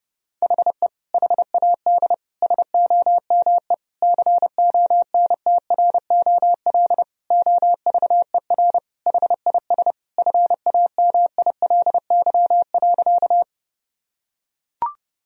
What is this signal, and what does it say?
Morse code, heard as a keyed tone, about 29 wpm, 700 Hz. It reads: HE HAD SOME CONTROL OVER HIS FAMILY.
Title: Sentences from Top 1000 Words - Review Part 3 - 30wpm